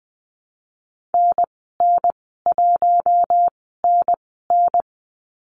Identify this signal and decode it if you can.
Morse code, heard as a keyed tone, about 20 wpm, spaced as for 14 wpm, 700 Hz. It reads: NN1NN
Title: 042 Lesson 8g - Number 1 - Call Signs 20wpm